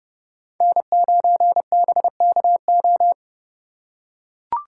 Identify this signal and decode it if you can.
Morse code, heard as a keyed tone, about 30 wpm, 700 Hz. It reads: N9BKO